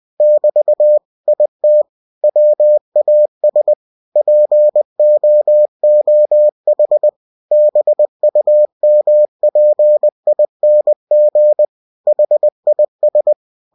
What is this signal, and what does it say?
Morse code, heard as a keyed tone, about 20 wpm, 600 Hz. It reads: <BT> IT WAS POOH BUMPING HIS
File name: cw-books / winnie-the-pooh / winnie-20wpm-page-0043